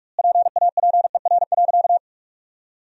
Morse code, heard as a keyed tone, about 46 wpm, 700 Hz. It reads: PAPER.
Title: 2x Word-Spacing - Sentences from Top 300 Words - 45wpm